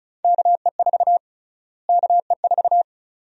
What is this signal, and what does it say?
Morse code, heard as a keyed tone, about 36 wpm, 700 Hz. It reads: KE4 KE4
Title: USA Call Sign Prefixes - Repeated 2x - 35wpm